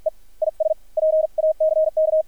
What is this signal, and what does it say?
Morse code, heard as a keyed tone, about 53 wpm, spaced as for 44 wpm, 630 Hz, with radio noise. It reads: E IS PACK
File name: archived / CRSnet